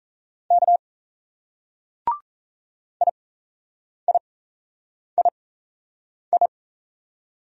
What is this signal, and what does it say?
Morse code, heard as a keyed tone, about 35 wpm, spaced as for 8 wpm, 700 Hz. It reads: K IIII